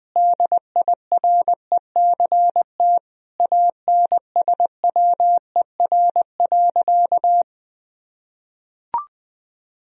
Morse code, heard as a keyed tone, about 20 wpm, 700 Hz. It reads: DIRECT ANSWER.
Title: Sentences from Top 1000 Words - Review Part 2 - 20wpm